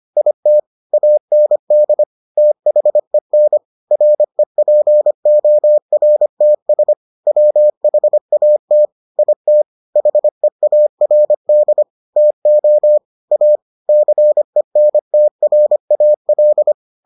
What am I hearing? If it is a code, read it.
Morse code, heard as a keyed tone, about 25 wpm, 600 Hz. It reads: IT AND THEN REPORTS WHAT IT HEARD TO A CENTRAL